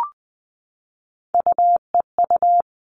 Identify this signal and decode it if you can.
Morse code, heard as a keyed tone, about 20 wpm, 700 Hz. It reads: UEU
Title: All Cut Numbers - 3-Digit 20wpm